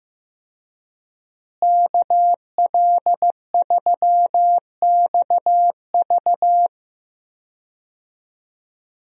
Morse code, heard as a keyed tone, about 15 wpm, 700 Hz. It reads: KL3XV
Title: USA Call Signs - 5 Characters - Repeated 3x - 15wpm